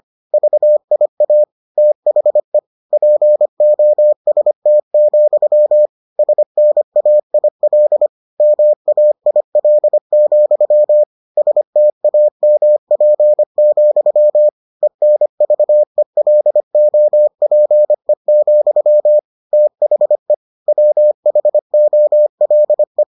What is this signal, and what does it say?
Morse code, heard as a keyed tone, about 25 wpm, 600 Hz. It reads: VIA THE POST, SNAIL MAIL, STAMP, ENVELOPE, THE WHOLE